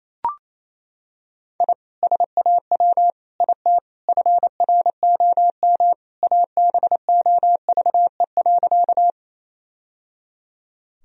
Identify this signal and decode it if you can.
Morse code, heard as a keyed tone, about 28 wpm, 700 Hz. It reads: I SAW IT FROM ABOVE.
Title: Sentences from Top 1000 Words - Review Part 2 - 28wpm